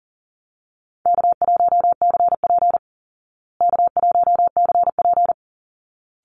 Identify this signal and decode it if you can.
Morse code, heard as a keyed tone, about 40 wpm, 700 Hz. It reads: K1CP K1CP
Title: Easy Call Signs Repeated 2x - 40wpm